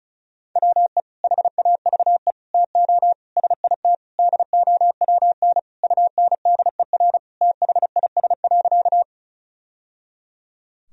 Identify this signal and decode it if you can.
Morse code, heard as a keyed tone, about 36 wpm, 700 Hz. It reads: WE HAVE TO SIT DOWN UNDER THIS.
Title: Sentences from Top 1000 Words - Review Part 1 - 35wpm